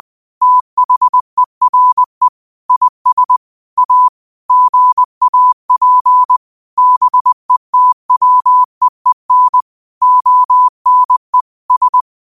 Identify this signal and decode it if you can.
Morse code, heard as a keyed tone, about 20 wpm, 1000 Hz. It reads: THERE IS A GAP BETWEEN ONE S